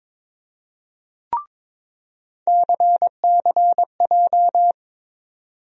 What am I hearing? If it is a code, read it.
Morse code, heard as a keyed tone, about 22 wpm, 700 Hz. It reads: CCJ
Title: USA Call Sign Postfixes - 3 Letters 22wpm